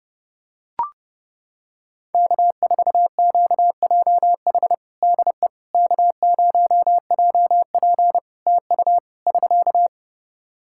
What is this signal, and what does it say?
Morse code, heard as a keyed tone, about 30 wpm, 700 Hz. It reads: K4QJH DE K0JP TU <SK>
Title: Practice QSO - Exchange 4 - Ending - 30wpm